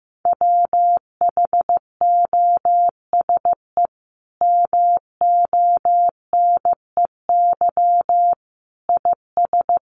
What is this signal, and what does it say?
Morse code, heard as a keyed tone, about 15 wpm, 700 Hz. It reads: WHOSE MONEY IS
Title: Sentences from Top 1200 Words 15wpm